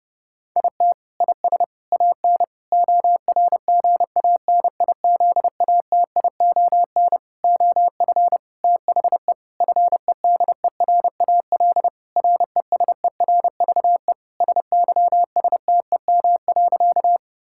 Morse code, heard as a keyed tone, about 30 wpm, 700 Hz. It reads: IT IS AN ORGANIZATION OF THE FEDERAL RESERVE SYSTEM.
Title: Sentences from Top 1000 Words - Review Part 6 - 30wpm